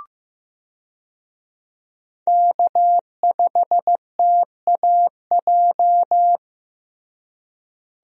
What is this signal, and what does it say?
Morse code, heard as a keyed tone, about 15 wpm, 700 Hz. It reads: K5TAJ
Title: USA Call Signs - 5 Characters - Repeated 3x - 15wpm